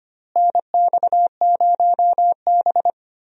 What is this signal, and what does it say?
Morse code, heard as a keyed tone, about 25 wpm, 700 Hz. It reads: NX0B
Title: USA Call Signs - 4 Characters - Repeated 3x - 25wpm